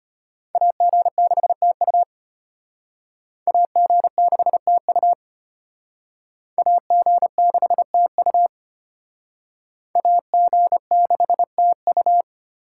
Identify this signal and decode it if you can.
Morse code, heard as a keyed tone, about 33 wpm, 700 Hz. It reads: AG6TU AG6TU AG6TU AG6TU